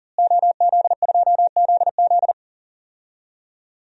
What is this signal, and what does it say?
Morse code, heard as a keyed tone, about 40 wpm, 700 Hz. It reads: OZ2ZZ
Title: DXCC-100 Call Signs - 5 Characters - Repeated 2x - 40wpm